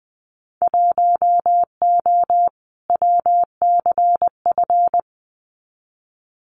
Morse code, heard as a keyed tone, about 20 wpm, 700 Hz. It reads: 1O WCF
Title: Winter Field Day Exchanges 20wpm